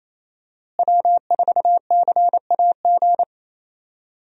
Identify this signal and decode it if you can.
Morse code, heard as a keyed tone, about 28 wpm, 700 Hz. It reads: W4CAG
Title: USA Call Signs - 5 Characters - Repeated 2x - 28wpm